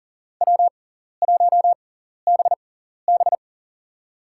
Morse code, heard as a keyed tone, about 40 wpm, spaced as for 14 wpm, 700 Hz. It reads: W1BB
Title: USA Call Signs - 4 Characters - 6x Character Spacing - 40wpm